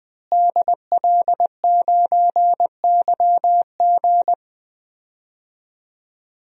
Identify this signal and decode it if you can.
Morse code, heard as a keyed tone, about 20 wpm, 700 Hz. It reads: DL9YG
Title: DXCC-100 Call Signs - 5 Characters - Repeated 2x - 20wpm